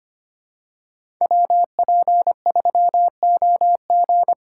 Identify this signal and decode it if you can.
Morse code, heard as a keyed tone, about 25 wpm, 700 Hz. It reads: WP3OG
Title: USA Call Signs - 5 Characters - 4x Character Spacing - 25wpm